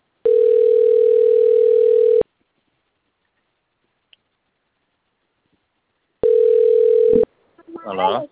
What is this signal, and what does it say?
Ringback tone (North America), 440 + 480 Hz, with speech.
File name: autotestagent / 11-54-46.136..callcentertransfer. .d9132fdc-01b8-4bc0-9523